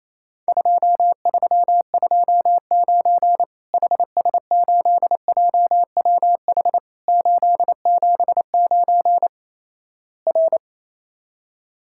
Morse code, heard as a keyed tone, about 28 wpm, 700 Hz. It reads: 2329 HS8JWH 879 R
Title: Worked All Europe DX Contest - QTC Exchange - 28wpm